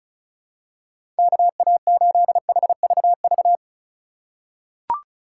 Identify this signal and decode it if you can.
Morse code, heard as a keyed tone, about 36 wpm, 700 Hz. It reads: KA8HVV
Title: USA Call Signs - 4x Character Spacing - 35wpm